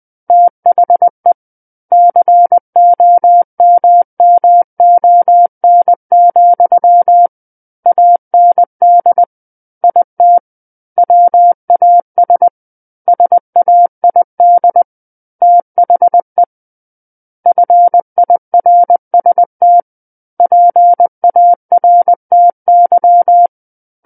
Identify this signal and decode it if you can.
Morse code, heard as a keyed tone, about 20 wpm, 700 Hz. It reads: THE COMMON, AND IT WAS SAID THE FIRST PARTY
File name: 20 WPM